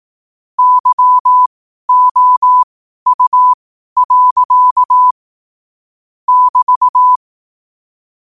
Morse code, heard as a keyed tone, about 18 wpm, spaced as for 12 wpm, 1000 Hz. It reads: YOU. <BT>
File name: 12wpm